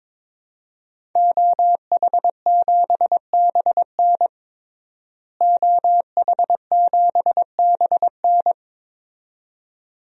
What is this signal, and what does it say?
Morse code, heard as a keyed tone, about 21 wpm, 700 Hz. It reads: OH7BN OH7BN
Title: DXCC-100 Call Signs - 5 Characters - Repeated 2x - 22wpm